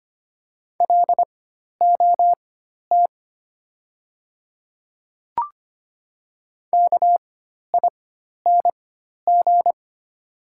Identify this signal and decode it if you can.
Morse code, heard as a keyed tone, about 25 wpm, spaced as for 12 wpm, 690 Hz. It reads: LOT KING
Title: 4x Character Spacing - Top 500 Words - 25wpm